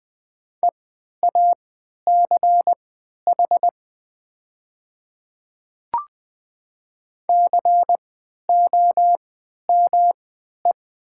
Morse code, heard as a keyed tone, about 20 wpm, spaced as for 11 wpm, 700 Hz. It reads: EACH COME
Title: 3x Character Spacing - Top 100 Words - 20wpm